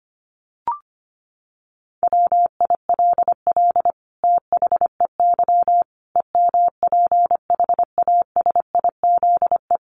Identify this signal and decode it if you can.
Morse code, heard as a keyed tone, about 25 wpm, 700 Hz. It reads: WILL THEY EMPHASIZE
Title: Sentences from Top 1500 Words 25wpm